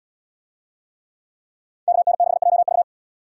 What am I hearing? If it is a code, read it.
Morse code, heard as a keyed tone, about 71 wpm, 700 Hz. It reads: KN6JV